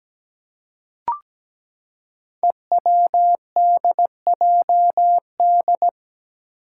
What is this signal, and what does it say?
Morse code, heard as a keyed tone, about 17 wpm, 700 Hz. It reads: EWDJD
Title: Five Challenging Letter-Number Combinations 17wpm